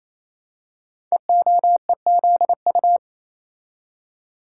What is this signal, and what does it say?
Morse code, heard as a keyed tone, about 28 wpm, 700 Hz. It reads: EOEZU
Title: Code Groups - 5 Letters - 28wpm